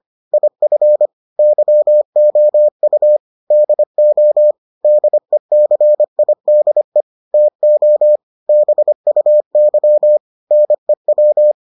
Morse code, heard as a keyed tone, about 25 wpm, 600 Hz. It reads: IF YOU DO DECIDE TO BUY NEW